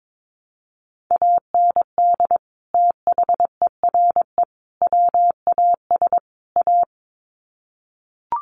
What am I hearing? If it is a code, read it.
Morse code, heard as a keyed tone, about 22 wpm, 700 Hz. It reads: AND THERE WAS A
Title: Sets of 4 Words 22wpm